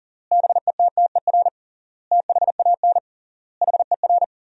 Morse code, heard as a keyed tone, about 37 wpm, 700 Hz. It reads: BETTER THAN HER
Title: 3x Word-Spacing - Sentences from Top 400 Words - 40wpm